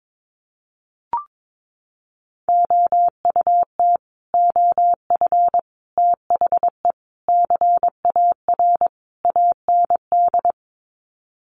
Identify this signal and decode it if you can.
Morse code, heard as a keyed tone, about 22 wpm, 700 Hz. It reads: OUT OF THE CAR AND
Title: Sets of 5 Words Encore 22wpm